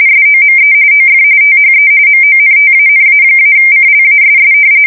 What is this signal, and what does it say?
RTTY, about 45.45 baud. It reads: SOME OF THE MONTANA-CLASS
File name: fsk170 / battleshipnj